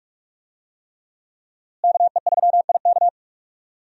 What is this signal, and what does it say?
Morse code, heard as a keyed tone, about 42 wpm, 700 Hz. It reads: KE3IK
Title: USA Call Signs - 5 Characters - Repeated 3x - 45wpm